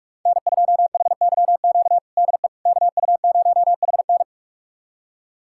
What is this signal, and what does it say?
Morse code, heard as a keyed tone, about 44 wpm, 700 Hz. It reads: N2HYQ DE KU0HN
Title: USA Callsign DE Callsign - 5 Characters - 45wpm